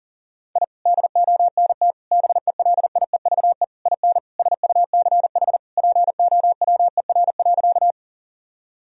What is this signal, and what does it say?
Morse code, heard as a keyed tone, about 40 wpm, 700 Hz. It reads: I DONT BELIEVE IN SUCH POWER.
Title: Sentences from Top 1000 Words - Review Part 2 - 40wpm